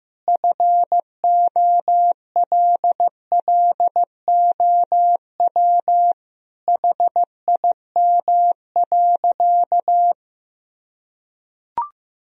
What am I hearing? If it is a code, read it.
Morse code, heard as a keyed tone, about 16 wpm, 700 Hz. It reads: FOLLOW HIM.
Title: Sentences from Top 1000 Words - Review Part 3 - 15wpm